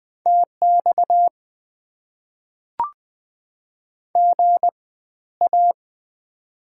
Morse code, heard as a keyed tone, about 20 wpm, spaced as for 13 wpm, 690 Hz. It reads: TX GA
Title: Top 90 QSO Elements - 4x Character Spacing - 20wpm